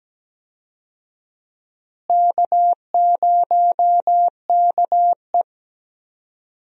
Morse code, heard as a keyed tone, about 17 wpm, 700 Hz. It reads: K0KE
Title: USA Call Signs - 4 Characters - Repeated 2x - 17wpm